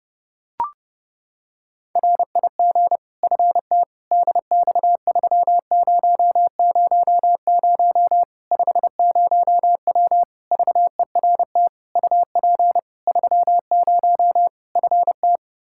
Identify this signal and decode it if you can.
Morse code, heard as a keyed tone, about 32 wpm, 700 Hz. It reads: RIG FTDX3000 50W VERT UP 30 FT
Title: Practice QSO - Exchange 2 - Info - 30wpm